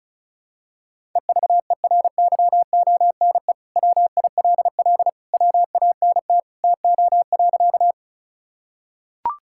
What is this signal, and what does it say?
Morse code, heard as a keyed tone, about 34 wpm, 700 Hz. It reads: EVERYONE WILL WANT TO.
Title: Sentences from Top 1000 Words - Review Part 5 - 35wpm